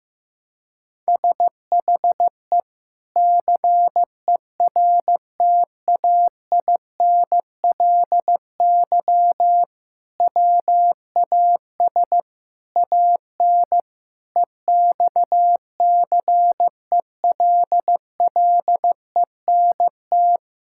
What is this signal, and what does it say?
Morse code, heard as a keyed tone, about 15 wpm, 700 Hz. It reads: SHE CERTAINLY WAS AN EXCELLENT
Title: Sentences from Top 1000 Words - Review Part 6 - 15wpm